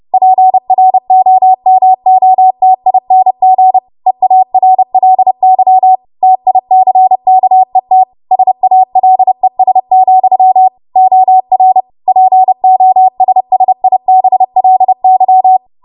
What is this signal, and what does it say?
Morse code, heard as a keyed tone, about 30 wpm, 750 Hz. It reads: PROMOTING EARLY TICKET SALES, OR POSSIBLY
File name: Morse / 30WPM